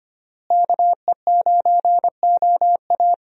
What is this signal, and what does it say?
Morse code, heard as a keyed tone, about 25 wpm, 700 Hz. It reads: KE9OA
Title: USA Call Signs - 5 Characters - Repeated 3x - 25wpm